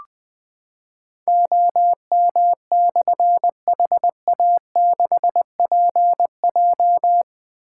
Morse code, heard as a keyed tone, about 20 wpm, 700 Hz. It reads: OM/HA6PJ
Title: Difficult Call Signs 20wpm